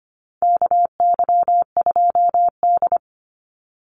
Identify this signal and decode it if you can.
Morse code, heard as a keyed tone, about 25 wpm, 700 Hz. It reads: KY2D
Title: Call Signs - 25wpm